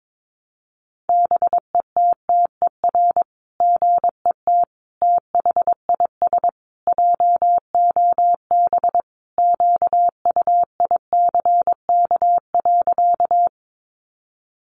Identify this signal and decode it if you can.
Morse code, heard as a keyed tone, about 22 wpm, 700 Hz. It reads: BETTER GET THIS JOB QUICK.
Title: Sentences from Top 600 Words 22wpm